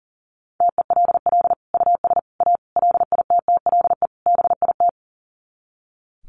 Morse code, heard as a keyed tone, about 40 wpm, 700 Hz. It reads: TELL US A LITTLE BIT
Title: Sets of 5 Words Encore2 40wpm